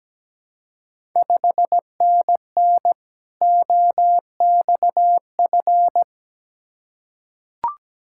Morse code, heard as a keyed tone, about 17 wpm, 700 Hz. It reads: 5NN OXF
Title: Ontario QSO Party Exchange 17wpm